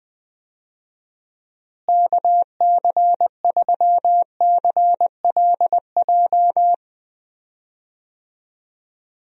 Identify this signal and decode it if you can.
Morse code, heard as a keyed tone, about 20 wpm, 700 Hz. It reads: KC3CLJ
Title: USA Call Signs - 6 Characters - Repeated 3x - 20wpm